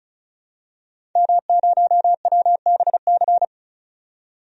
Morse code, heard as a keyed tone, about 36 wpm, 700 Hz. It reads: M0WBC
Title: DXCC-100 Call Signs - 5 Characters - 35wpm